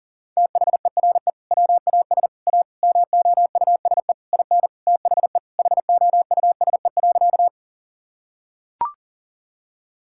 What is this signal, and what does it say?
Morse code, heard as a keyed tone, about 40 wpm, 700 Hz. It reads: THERE WAS A MOUSE IN THE HOUSE.
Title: Sentences from Top 1000 Words - Review Part 5 - 40wpm